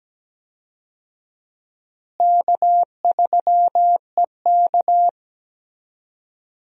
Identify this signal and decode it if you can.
Morse code, heard as a keyed tone, about 17 wpm, 700 Hz. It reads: K3EK
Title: USA Call Signs - 4 Characters - Repeated 3x - 17wpm